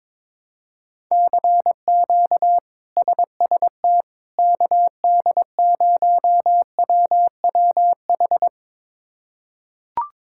Morse code, heard as a keyed tone, about 22 wpm, 700 Hz. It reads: CQ SST KD0WWH
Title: K1USN SST QSO Practice - 22wpm